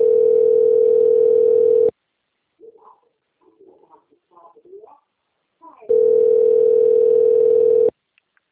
Ringback tone (North America), 440 + 480 Hz.
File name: rg-1600-unknown- - - .37969